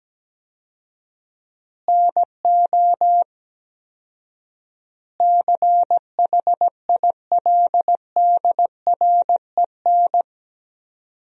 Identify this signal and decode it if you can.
Morse code, heard as a keyed tone, about 17 wpm, 700 Hz. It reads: NO CHILDREN
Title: 4x Word-Spacing - Sentences from Top 400 Words - 17wpm